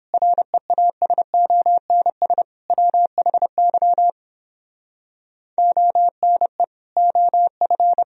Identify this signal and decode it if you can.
Morse code, heard as a keyed tone, about 29 wpm, 700 Hz. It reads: REASONS WHY ONE OF